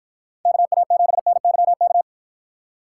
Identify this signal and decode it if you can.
Morse code, heard as a keyed tone, about 60 wpm, 700 Hz. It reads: KA7NYK